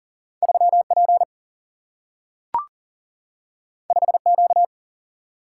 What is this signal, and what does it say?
Morse code, heard as a keyed tone, about 40 wpm, 700 Hz. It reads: 3 P 5 Q
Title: Single-Letter Single-Number 40wpm